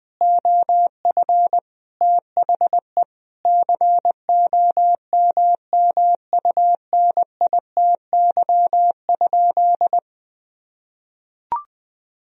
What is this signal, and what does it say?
Morse code, heard as a keyed tone, about 20 wpm, 700 Hz. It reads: OF THE COMMUNITY?
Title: Sentences from Top 1000 Words - Review Part 4 - 20wpm